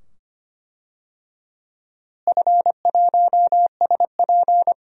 Morse code, heard as a keyed tone, about 25 wpm, 700 Hz. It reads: F1SP
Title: DXCC-100 Call Signs - 4 Characters - Repeated 2x - 25wpm